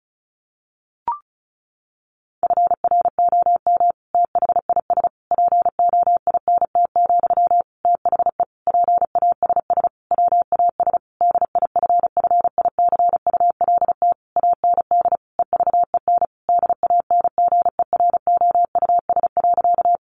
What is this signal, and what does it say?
Morse code, heard as a keyed tone, about 34 wpm, 700 Hz. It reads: FROM THIS POINT, THE PASS WAS DIFFICULT AND EVEN DANGEROUS.
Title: Sentences from Top 1200 Words 35wpm